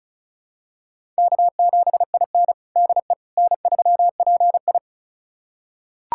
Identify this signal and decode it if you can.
Morse code, heard as a keyed tone, about 36 wpm, 700 Hz. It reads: K7IN DE N3PI
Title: USA Callsign DE Callsign - 4 Characters - 35wpm